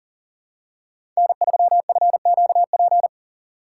Morse code, heard as a keyed tone, about 40 wpm, 700 Hz. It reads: N3FQP
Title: USA Call Signs - 5 Characters - 4x Character Spacing - 40wpm